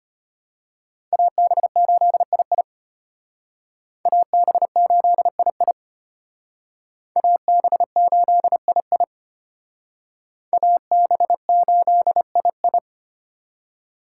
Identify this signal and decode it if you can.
Morse code, heard as a keyed tone, about 32 wpm, 700 Hz. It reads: AB8II AB8II AB8II AB8II